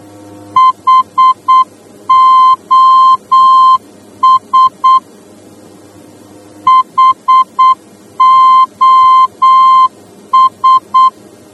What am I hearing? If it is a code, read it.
Morse code, heard as a keyed tone, about 8 wpm, 1040 Hz. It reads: HOS HOS
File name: germany / mx / HOS-D